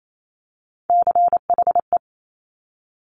Morse code, heard as a keyed tone, about 28 wpm, 700 Hz. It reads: CHE
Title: Top Three Letter Combinations in Words 28wpm